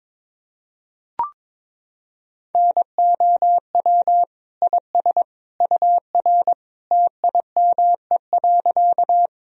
Morse code, heard as a keyed tone, about 22 wpm, 700 Hz. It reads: NOW IS UR TIME.
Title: Sentences from Top 100 Words and Abbreviations 22wpm